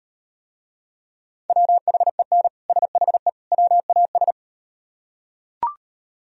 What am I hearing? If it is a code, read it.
Morse code, heard as a keyed tone, about 38 wpm, 700 Hz. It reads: WHEN SHE WAS